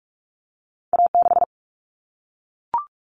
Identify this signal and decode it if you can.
Morse code, heard as a keyed tone, about 44 wpm, 700 Hz. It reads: A6
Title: Common Cut Numbers - 2-Digit 45wpm